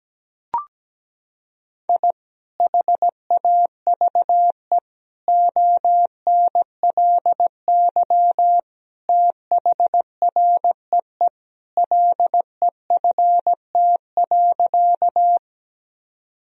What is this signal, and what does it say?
Morse code, heard as a keyed tone, about 17 wpm, 700 Hz. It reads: I HAVE ONLY THREE LEFT.
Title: Sentences from Top 1000 Words - Review Part 1 - 17wpm